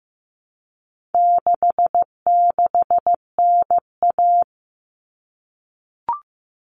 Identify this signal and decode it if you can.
Morse code, heard as a keyed tone, about 15 wpm, 700 Hz. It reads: 66NA
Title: All Cut Numbers - 4-Digit 15wpm